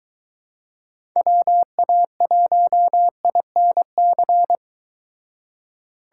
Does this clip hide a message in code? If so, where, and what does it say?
Yes: Morse code from 1.2 to 4.5 s, heard as a keyed tone, about 23 wpm, 700 Hz. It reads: WA1INC